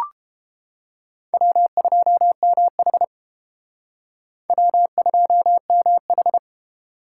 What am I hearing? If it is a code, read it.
Morse code, heard as a keyed tone, about 29 wpm, 700 Hz. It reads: W2MH W2MH